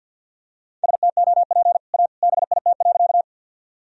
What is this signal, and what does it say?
Morse code, heard as a keyed tone, about 49 wpm, 700 Hz. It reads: STOP A BIT.
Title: Sentences from Top 1000 Words - Review Part 1 - 50wpm